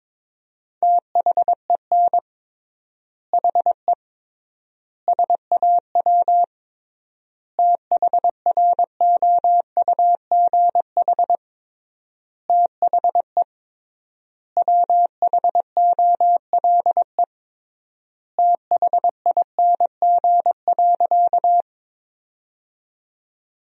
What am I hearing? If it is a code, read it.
Morse code, heard as a keyed tone, about 22 wpm, 700 Hz. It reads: THEN HE SAW THROUGH THE WHOLE THING.
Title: 3x Word-Spacing - Sentences from Top 400 Words - 22wpm